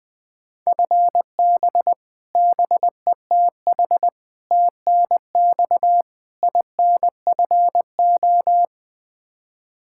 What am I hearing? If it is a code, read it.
Morse code, heard as a keyed tone, about 20 wpm, 700 Hz. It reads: FB BETH TNX INFO
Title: Practice QSO - Exchange 3 - More Info - 20wpm